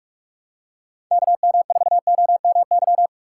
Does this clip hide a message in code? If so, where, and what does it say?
Yes: Morse code from 1.1 to 3.1 s, heard as a keyed tone, about 44 wpm, 700 Hz. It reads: KM4OMY